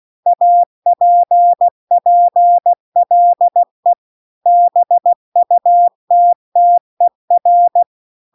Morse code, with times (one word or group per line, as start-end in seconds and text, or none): 0.3-3.9 APPLE
4.5-7.8 BUTTER